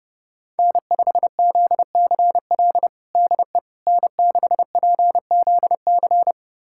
Morse code, heard as a keyed tone, about 32 wpm, 700 Hz. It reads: N5ZCL DE N6PZC
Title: USA Callsign DE Callsign 30wpm